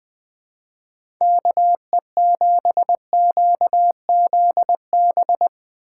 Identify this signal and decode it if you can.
Morse code, heard as a keyed tone, about 21 wpm, 700 Hz. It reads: KE7QZB
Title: USA Call Signs - 3x Character Spacing - 20wpm